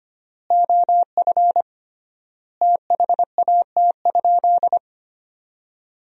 Morse code, heard as a keyed tone, about 25 wpm, 700 Hz. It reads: OF THAT?
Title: 3x Word-Spacing - Sentences from Top 100 Words - 25wpm